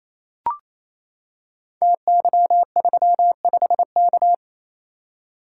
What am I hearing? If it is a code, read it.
Morse code, heard as a keyed tone, about 29 wpm, 700 Hz. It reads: TY35K
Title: Code Groups - 5 Letter:Numbers - 28wpm